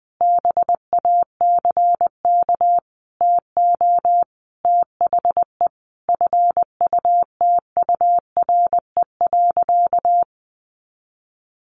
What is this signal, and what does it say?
Morse code, heard as a keyed tone, about 20 wpm, 700 Hz. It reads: BACK TO THE FUTURE.
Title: A Sound Of Thunder 20wpm